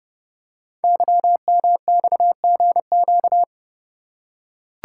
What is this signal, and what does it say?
Morse code, heard as a keyed tone, about 30 wpm, 700 Hz. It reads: YMXGQ
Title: Code Groups - 5 Letters - 30wpm